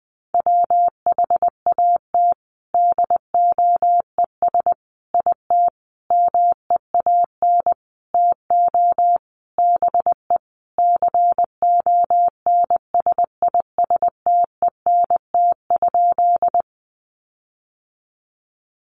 Morse code, heard as a keyed tone, about 20 wpm, 700 Hz. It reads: WHAT DOES IT MEAN TO BE CONSISTENT?
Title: Sentences from Top 1500 Words 20wpm